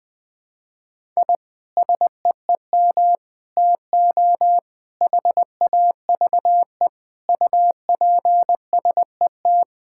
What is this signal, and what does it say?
Morse code, heard as a keyed tone, about 20 wpm, 700 Hz. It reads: I SEEM TO HAVE UPSET
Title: Sentences from Top 1000 Words - Review Part 5 - 20wpm